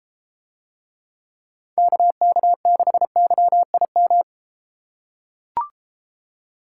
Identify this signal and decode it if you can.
Morse code, heard as a keyed tone, about 32 wpm, 700 Hz. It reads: KK6YIM